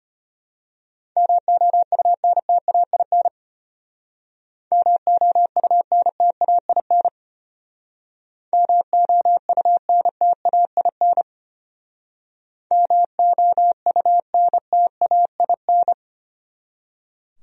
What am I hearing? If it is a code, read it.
Morse code, heard as a keyed tone, about 32 wpm, 700 Hz. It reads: MOUNTAIN MOUNTAIN MOUNTAIN MOUNTAIN